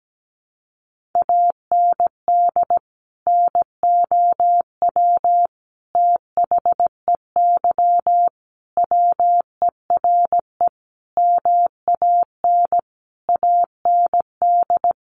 Morse code, heard as a keyed tone, about 17 wpm, 700 Hz. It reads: AND NOW THEY WERE MAN AND
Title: Sentences from Top 1100 Words 17wpm